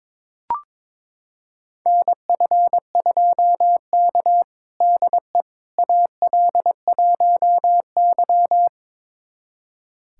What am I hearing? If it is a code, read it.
Morse code, heard as a keyed tone, about 21 wpm, 700 Hz. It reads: NF2K DE AL1Y
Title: USA Callsign DE Callsign - 4 Characters - 22wpm